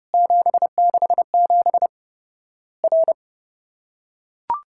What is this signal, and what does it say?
Morse code, heard as a keyed tone, about 30 wpm, 700 Hz. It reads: 767 R
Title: Worked All Europe DX Contest - QTC Exchange - 30wpm